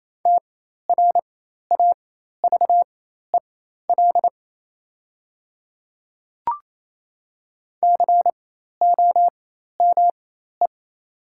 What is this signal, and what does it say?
Morse code, heard as a keyed tone, about 28 wpm, spaced as for 13 wpm, 700 Hz. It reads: TRAVEL COME
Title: 4x Character Spacing - Top 500 Words - 28wpm